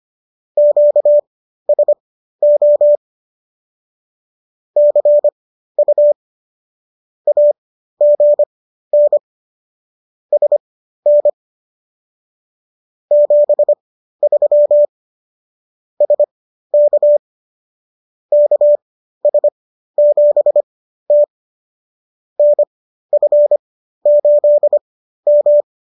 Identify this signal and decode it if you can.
Morse code, heard as a keyed tone, about 25 wpm, spaced as for 13 wpm, 590 Hz. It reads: QSO CU AGN SN 73 SK KS7T NF8M